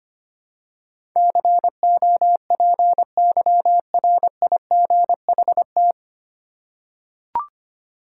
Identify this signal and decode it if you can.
Morse code, heard as a keyed tone, about 25 wpm, 700 Hz. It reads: COPYRIGHT
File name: practice / 25wpm / 9 Letter Words - First 3 Letters Given - 25wpm